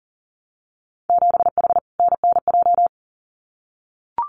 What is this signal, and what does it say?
Morse code, heard as a keyed tone, about 40 wpm, 700 Hz. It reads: 7H NNJ
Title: Winter Field Day Exchanges Encore 40wpm